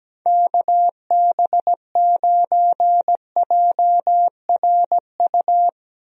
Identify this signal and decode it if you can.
Morse code, heard as a keyed tone, about 17 wpm, 700 Hz. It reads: KB9JRU
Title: USA Call Signs - 6 Characters - Repeated 2x - 17wpm